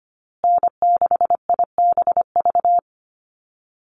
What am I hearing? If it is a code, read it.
Morse code, heard as a keyed tone, about 25 wpm, 700 Hz. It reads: N6IBV
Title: USA Call Signs Repeated 2x 25wpm